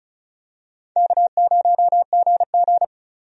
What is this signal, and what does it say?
Morse code, heard as a keyed tone, about 36 wpm, 700 Hz. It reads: K0GG
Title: USA Call Signs - 4 Characters - 35wpm